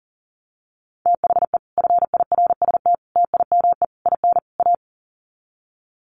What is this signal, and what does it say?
Morse code, heard as a keyed tone, about 40 wpm, 700 Hz. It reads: THE FIRST TIME IN A
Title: Sets of 5 Words Encore 40wpm